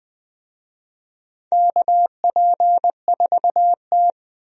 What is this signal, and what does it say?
Morse code, heard as a keyed tone, about 20 wpm, 700 Hz. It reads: KP4T
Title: DXCC-100 Call Signs - 4 Characters - 20wpm